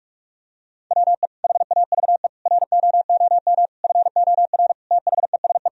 Morse code, heard as a keyed tone, about 41 wpm, 700 Hz. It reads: WE HAVE ROOM FOR THESE
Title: Sentences from Top 1000 Words - Review Part 1 - 45wpm